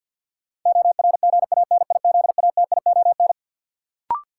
Morse code, heard as a keyed tone, about 52 wpm, 700 Hz. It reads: ORGANIZATION
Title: Sentences from Top 1000 Words - Review Part 5 - New Word Drill - 50wpm